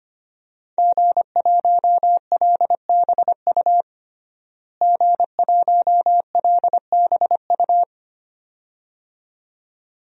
Morse code, heard as a keyed tone, about 25 wpm, 700 Hz. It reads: G1LBU G1LBU
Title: DXCC-100 Call Signs - 5 Characters - Repeated 2x - 25wpm